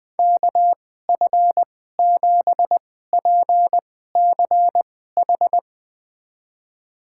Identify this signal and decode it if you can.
Morse code, heard as a keyed tone, about 20 wpm, spaced as for 14 wpm, 700 Hz. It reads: KF7PCH
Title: USA Call Signs - 6 Characters - 2x Character Spacing - 20wpm